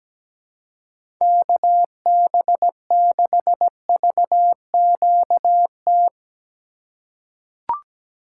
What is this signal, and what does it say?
Morse code, heard as a keyed tone, about 17 wpm, 700 Hz. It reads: KB6VQT
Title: USA Call Signs - 6 Characters - 4x Character Spacing - 17wpm